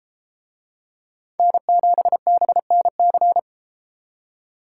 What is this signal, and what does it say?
Morse code, heard as a keyed tone, about 32 wpm, 700 Hz. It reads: N7BNC